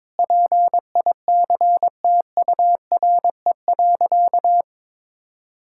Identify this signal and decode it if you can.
Morse code, heard as a keyed tone, about 22 wpm, 700 Hz. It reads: PICTURE.
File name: practice / 22wpm / Sentences with 4-Word Phrases - 22wpm